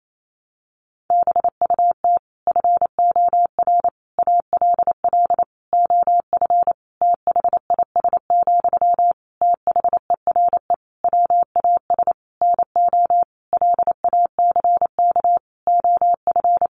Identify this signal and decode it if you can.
Morse code, heard as a keyed tone, about 28 wpm, 700 Hz. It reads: BUT FOR ALL OF THIS, THERE WAS NO LACK OF
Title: Sentences from Top 1300 Words 28wpm